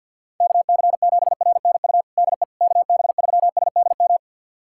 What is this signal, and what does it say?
Morse code, heard as a keyed tone, about 51 wpm, 700 Hz. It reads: KC7RNU DE KB3SDM
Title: USA Callsign DE Callsign - 6 Characters - 50wpm